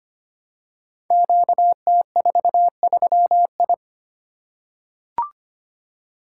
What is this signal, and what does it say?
Morse code, heard as a keyed tone, about 25 wpm, 700 Hz. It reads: QT43I
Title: Code Groups - 5 Letter:Numbers - 25wpm